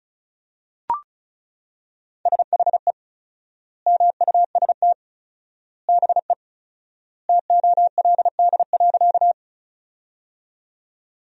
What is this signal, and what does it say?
Morse code, heard as a keyed tone, about 34 wpm, 700 Hz. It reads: SHE MUST BE TOLD.
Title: 4x Word-Spacing - Sentences from Top 400 Words - 35wpm